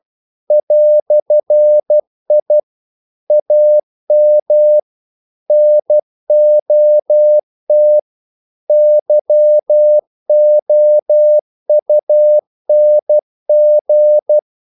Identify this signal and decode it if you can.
Morse code, heard as a keyed tone, about 12 wpm, 600 Hz. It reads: "I AM NOT YOUNG
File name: CW / Practice / EnglishQuotations0312